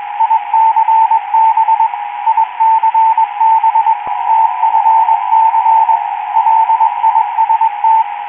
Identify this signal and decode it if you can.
Morse code, heard as a keyed tone, about 21 wpm, 890 Hz, with radio noise. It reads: ICB ICB TFC LIST